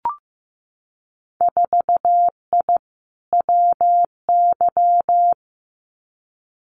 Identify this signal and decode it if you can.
Morse code, heard as a keyed tone, about 15 wpm, 700 Hz. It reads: 4I WY
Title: Winter Field Day Exchanges Encore 15wpm